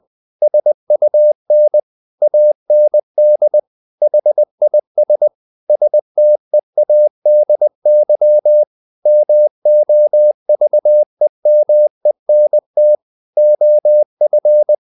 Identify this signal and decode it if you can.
Morse code, heard as a keyed tone, about 20 wpm, 600 Hz. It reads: SUN AND HIS STEADY MOVEMENT OF
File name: the old man and the sea / the old man and the sea part 1 20wpm